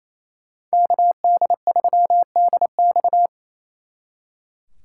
Morse code, heard as a keyed tone, about 36 wpm, 700 Hz. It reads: KD3DX